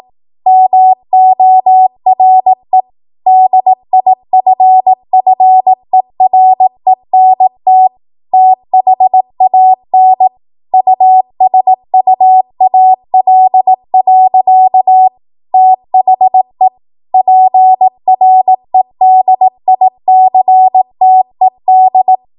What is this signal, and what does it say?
Morse code, heard as a keyed tone, about 18 wpm, 750 Hz. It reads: MORE DIFFERENT THAN USUAL. THE PREDICTED
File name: Morse / 18WPM